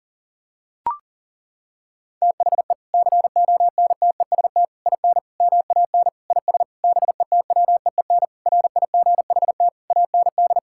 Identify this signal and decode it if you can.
Morse code, heard as a keyed tone, about 40 wpm, 700 Hz. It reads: THE CONTEST IN MAN IS BETWEEN RIGHT AND
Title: Sentences from Top 1600 Words 40wpm